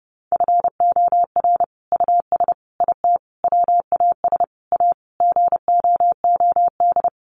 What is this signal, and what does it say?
Morse code, heard as a keyed tone, about 30 wpm, 700 Hz. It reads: FOR US IT WAS A GOOD
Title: Sentences from Top 1200 Words 30wpm